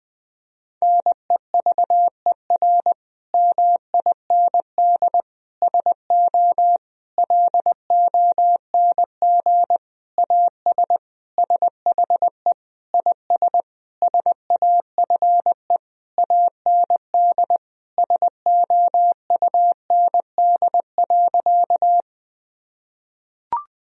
Morse code, heard as a keyed tone, about 19 wpm, 700 Hz. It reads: NEVER MIND SO LONG AS SHE IS SAFE AND SOUND.
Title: Sentences from Top 1000 Words - Review Part 1 - 20wpm